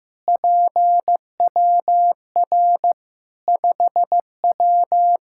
Morse code, heard as a keyed tone, about 16 wpm, 700 Hz. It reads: PWR 5W
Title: K1USN SST QSO Practice 15wpm